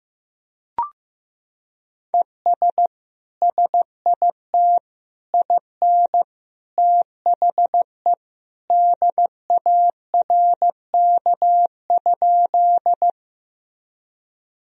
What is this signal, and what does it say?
Morse code, heard as a keyed tone, about 16 wpm, 700 Hz. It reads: ES SIT IN THE DARK?
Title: Sentences from Top 500 Words and Abbreviations 15wpm